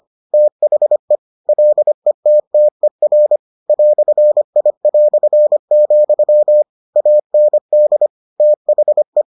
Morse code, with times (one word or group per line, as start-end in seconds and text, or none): 0.3-1.2 THE
1.5-3.4 LETTER
3.7-6.6 "I",
7.0-8.1 AND
8.4-9.2 THE